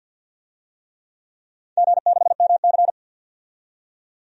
Morse code, heard as a keyed tone, about 53 wpm, 700 Hz. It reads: G6MC
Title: DXCC-100 Call Signs - 4 Characters - 50wpm